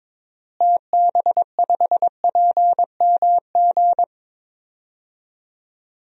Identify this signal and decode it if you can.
Morse code, heard as a keyed tone, about 22 wpm, 700 Hz. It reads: TB5PMG
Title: DXCC-100 Call Signs - 6 Characters - Repeated 3x - 22wpm